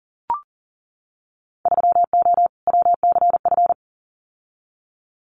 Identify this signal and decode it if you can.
Morse code, heard as a keyed tone, about 42 wpm, 700 Hz. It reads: 3O WCF
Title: Winter Field Day Exchanges 40wpm